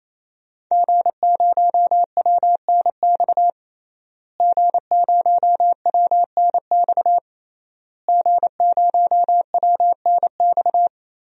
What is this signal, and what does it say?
Morse code, heard as a keyed tone, about 27 wpm, 700 Hz. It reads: G0WNX G0WNX G0WNX
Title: DXCC-100 Call Signs - 5 Characters - Repeated 3x - 28wpm